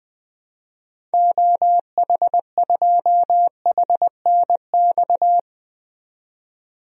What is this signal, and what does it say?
Morse code, heard as a keyed tone, about 21 wpm, 700 Hz. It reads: OH2HNX
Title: DXCC-100 Call Signs - 6 Characters - 20wpm